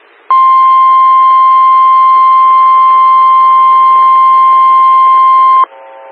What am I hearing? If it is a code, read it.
Emergency Alert System (SAME) tones (the 1050 Hz attention tone). It is heard from 0.3 to 5.7 s.
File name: WX9MCS 147255 MHz SKYWARN Repeater